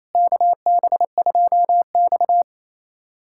Morse code, heard as a keyed tone, about 28 wpm, 700 Hz. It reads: KB2X
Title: USA Callsign DE Callsign - 4 Characters - 28wpm